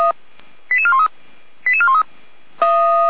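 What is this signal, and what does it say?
Five-tone selective call (ZVEI), heard as quick runs of five tones, tones of about 70 ms. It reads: 88413 88413
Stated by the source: ZVEI five-tone selective call 88171, received by SDR.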